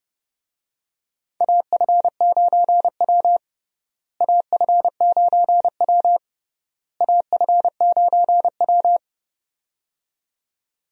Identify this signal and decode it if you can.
Morse code, heard as a keyed tone, about 30 wpm, 700 Hz. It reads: AF9W AF9W AF9W
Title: USA Call Signs - 4 Characters - Repeated 3x - 30wpm